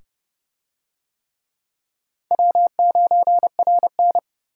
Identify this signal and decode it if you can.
Morse code, heard as a keyed tone, about 25 wpm, 700 Hz. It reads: W9RN